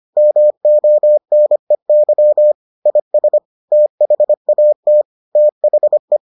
Morse code, heard as a keyed tone, about 25 wpm, 600 Hz. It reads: MONEY IS THAT THE